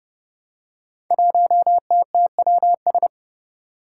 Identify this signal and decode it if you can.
Morse code, heard as a keyed tone, about 30 wpm, 700 Hz. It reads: 1TTWS
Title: Five Challenging Letter-Number Combinations 30wpm